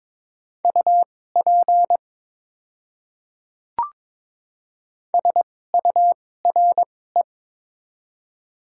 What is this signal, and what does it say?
Morse code, heard as a keyed tone, about 23 wpm, spaced as for 16 wpm, 700 Hz. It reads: UP SURE
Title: 2x Character Spacing - Top 300 Words - 22wpm